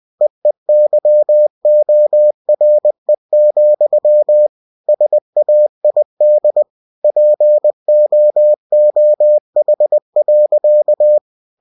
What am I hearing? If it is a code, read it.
Morse code, heard as a keyed tone, about 20 wpm, 600 Hz. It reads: EEYORE, SAID POOH.